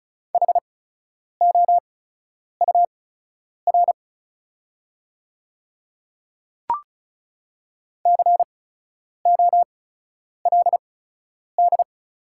Morse code, heard as a keyed tone, about 36 wpm, spaced as for 10 wpm, 690 Hz. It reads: HOUR COLD
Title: 8x Character Spacing - Top 500 Words - 35wpm